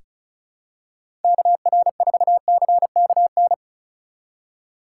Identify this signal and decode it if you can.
Morse code, heard as a keyed tone, about 34 wpm, 700 Hz. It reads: KR4CKN
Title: USA Call Signs - 6 Characters - 2x Character Spacing - 35wpm